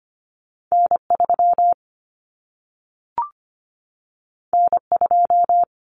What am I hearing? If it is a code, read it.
Morse code, heard as a keyed tone, about 25 wpm, 700 Hz. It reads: N3 N2
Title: Common Cut Numbers - 2-Digit 25wpm